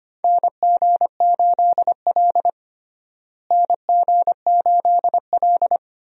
Morse code, heard as a keyed tone, about 25 wpm, 700 Hz. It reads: NG8L NG8L
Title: USA Call Signs - 4 Characters - Repeated 3x - 25wpm